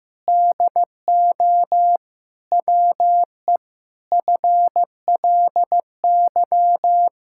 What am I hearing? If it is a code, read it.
Morse code, heard as a keyed tone, about 16 wpm, 700 Hz. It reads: DO WE FLY
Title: Pangram Challenge v2 - 15wpm